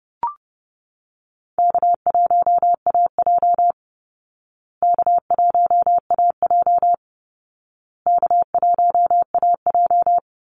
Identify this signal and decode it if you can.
Morse code, heard as a keyed tone, about 30 wpm, 700 Hz. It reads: K1AJ K1AJ K1AJ
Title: Easy Call Signs Repeated 3x - 30wpm